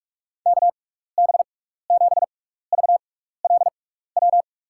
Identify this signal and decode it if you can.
Morse code, heard as a keyed tone, about 42 wpm, spaced as for 15 wpm, 700 Hz. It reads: KB7VLW
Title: USA Call Signs - 6 Characters - 6x Character Spacing - 45wpm